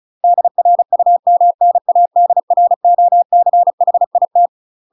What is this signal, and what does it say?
Morse code, heard as a keyed tone, about 36 wpm, 700 Hz. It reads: DRUMNADROCHIT
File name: UK MASTER QSOs TRAINING / 35WPM / UK MASTER QSOs 0044